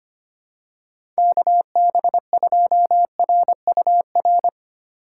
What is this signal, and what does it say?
Morse code, heard as a keyed tone, about 25 wpm, 700 Hz. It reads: KB2RUR
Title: USA Call Signs - 6 Characters - 6x Character Spacing - 25wpm